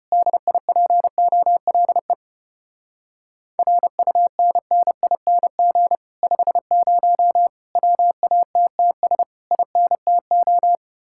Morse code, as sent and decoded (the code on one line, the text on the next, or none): -.. .. .--. --- .-.. / .-. ..- -. -. .. -. --. / ..... ----- / .-- .- - - ... / .. -. - ---
DIPOLE RUNNING 50 WATTS INTO